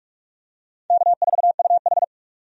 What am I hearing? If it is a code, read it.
Morse code, heard as a keyed tone, about 46 wpm, 700 Hz. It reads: K4UH
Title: USA Call Signs - 4 Characters - 45wpm